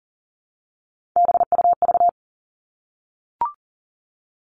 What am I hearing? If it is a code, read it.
Morse code, heard as a keyed tone, about 37 wpm, 700 Hz. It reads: BUV
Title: All Cut Numbers - 3-Digit 40wpm